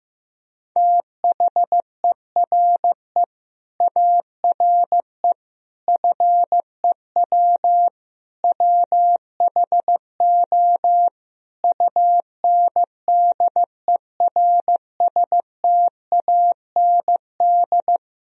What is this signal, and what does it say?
Morse code, heard as a keyed tone, about 15 wpm, 700 Hz. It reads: THERE ARE FEW WHO UNDERSTAND
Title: Sentences from Top 1000 Words - Review Part 1 - 15wpm